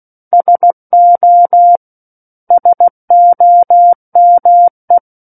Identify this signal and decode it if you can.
Morse code, heard as a keyed tone, about 16 wpm, 700 Hz. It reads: SO SOME
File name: Book One / 07-Chapter 7 - 16 WPM